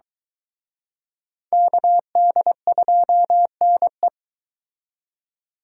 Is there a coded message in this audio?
Yes: Morse code from 1.5 to 4.1 s, heard as a keyed tone, about 23 wpm, 700 Hz. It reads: KD2NE